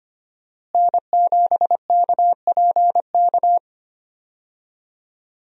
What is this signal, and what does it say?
Morse code, heard as a keyed tone, about 25 wpm, 700 Hz. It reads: N7KPK
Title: USA Call Signs - 5 Characters - Repeated 2x - 25wpm